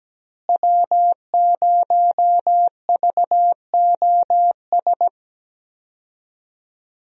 Morse code, heard as a keyed tone, about 17 wpm, 700 Hz. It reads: W0VOS
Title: USA Call Signs - 5 Characters - Repeated 3x - 17wpm